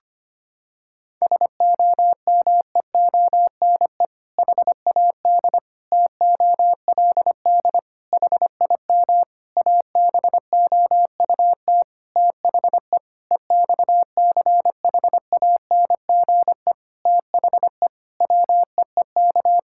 Morse code, heard as a keyed tone, about 25 wpm, 700 Hz. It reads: SOMEONE HAD TOLD HIM ABOUT THE EXCHANGE THE WEEK
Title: Sentences from Top 1000 Words - Review Part 6 - 25wpm